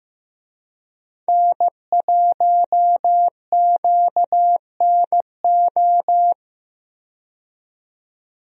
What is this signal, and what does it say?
Morse code, heard as a keyed tone, about 15 wpm, 700 Hz. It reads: N1QNO
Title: USA Call Signs - 5 Characters - 15wpm